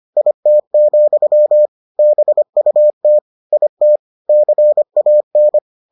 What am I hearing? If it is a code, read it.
Morse code, heard as a keyed tone, about 25 wpm, 600 Hz. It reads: IT, BUT IT CAN